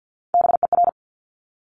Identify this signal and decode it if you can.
Morse code, heard as a keyed tone, about 50 wpm, 700 Hz. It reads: BER
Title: Top Three Letter Combinations in Words 50wpm